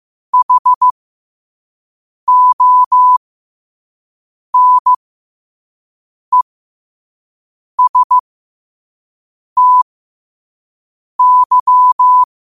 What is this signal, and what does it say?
Morse code, heard as a keyed tone, about 15 wpm, spaced as for 5 wpm, 1000 Hz. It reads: HONESTY
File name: qotdfiles / 5wpm